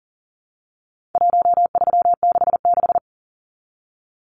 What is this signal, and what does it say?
Morse code, heard as a keyed tone, about 40 wpm, 700 Hz. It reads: 1366
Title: Numbers - Contest Style - 4-Digit 40wpm